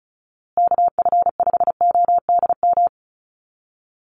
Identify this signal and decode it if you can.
Morse code, heard as a keyed tone, about 34 wpm, 700 Hz. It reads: KF5ODM
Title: USA Call Signs Repeated 3x 35wpm